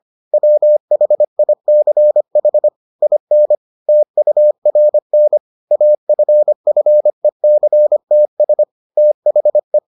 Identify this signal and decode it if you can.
Morse code, heard as a keyed tone, about 25 wpm, 600 Hz. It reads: WHICH IN TURN AFFECTS THE